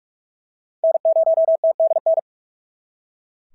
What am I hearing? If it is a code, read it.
Morse code, heard as a keyed tone, about 44 wpm, 690 Hz. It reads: N0TDN
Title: SOTA QSO Practice 45wpm